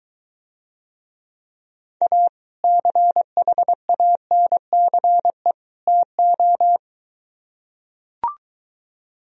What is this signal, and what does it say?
Morse code, heard as a keyed tone, about 23 wpm, 700 Hz. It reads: A CHANCE TO